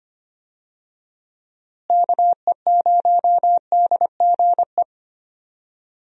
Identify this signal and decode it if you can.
Morse code, heard as a keyed tone, about 25 wpm, 700 Hz. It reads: KE0DGE
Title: USA Call Signs - 6 Characters - Repeated 2x - 25wpm